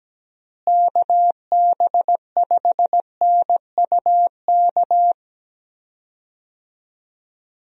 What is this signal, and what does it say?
Morse code, heard as a keyed tone, about 17 wpm, 700 Hz. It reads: KB5NUK
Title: USA Call Signs - 6 Characters - Repeated 2x - 17wpm